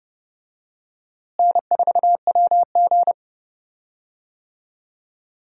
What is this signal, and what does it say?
Morse code, heard as a keyed tone, about 31 wpm, 700 Hz. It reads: N4WG
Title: USA Call Signs - 4 Characters - Repeated 2x - 30wpm